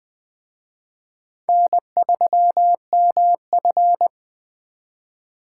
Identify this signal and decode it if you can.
Morse code, heard as a keyed tone, about 20 wpm, 700 Hz. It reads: N3MF
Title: Speed Racing - USA 1x2 Call Signs 17wpm